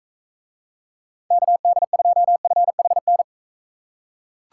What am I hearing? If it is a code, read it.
Morse code, heard as a keyed tone, about 38 wpm, 700 Hz. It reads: KD2FHN